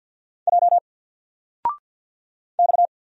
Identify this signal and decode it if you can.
Morse code, heard as a keyed tone, about 50 wpm, 700 Hz. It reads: J X
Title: Single Letter-Number 50wpm